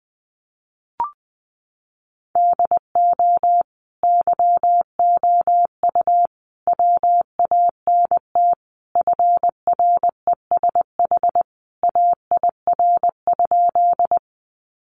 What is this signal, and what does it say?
Morse code, heard as a keyed tone, about 20 wpm, 700 Hz. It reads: DO YOU WANT FRESH AIR?
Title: Sentences from Top 1200 Words 20wpm